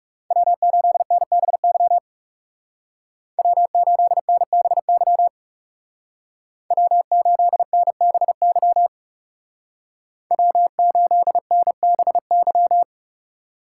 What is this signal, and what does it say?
Morse code, heard as a keyed tone, about 36 wpm, 700 Hz. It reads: W8NBY W8NBY W8NBY W8NBY
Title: Speed Racing - USA 1x3 Call Signs 30wpm